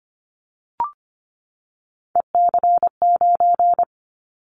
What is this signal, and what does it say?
Morse code, heard as a keyed tone, about 25 wpm, 700 Hz. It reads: EC9
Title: Callsign Prefixes for Top 100 DXCC - 25wpm